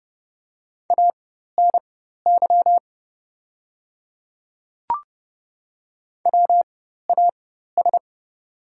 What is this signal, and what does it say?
Morse code, heard as a keyed tone, about 30 wpm, spaced as for 14 wpm, 700 Hz. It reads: ANY WAS
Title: 4x Character Spacing - Top 200 Words - 30wpm